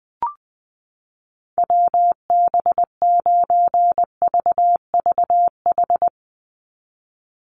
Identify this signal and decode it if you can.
Morse code, heard as a keyed tone, about 20 wpm, 700 Hz. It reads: WB9VVH
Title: USA Call Signs 20wpm